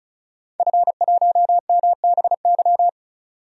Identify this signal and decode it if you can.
Morse code, heard as a keyed tone, about 36 wpm, 700 Hz. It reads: F1MBY
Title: DXCC-100 Call Signs - 5 Characters - Repeated 3x - 35wpm